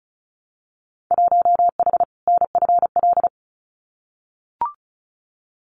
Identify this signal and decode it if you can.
Morse code, heard as a keyed tone, about 34 wpm, 700 Hz. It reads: 1H NFL
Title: Winter Field Day Exchanges 35wpm